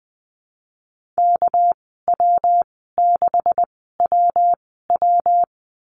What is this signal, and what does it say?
Morse code, heard as a keyed tone, about 20 wpm, spaced as for 15 wpm, 700 Hz. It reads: KW6WW
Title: 238 Lesson 38g - Letter q - Call Signs 20wpm